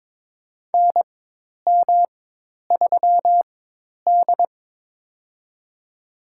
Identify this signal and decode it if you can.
Morse code, heard as a keyed tone, about 22 wpm, spaced as for 10 wpm, 700 Hz. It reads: NM3D
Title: USA Call Signs - 4x Character Spacing - 22wpm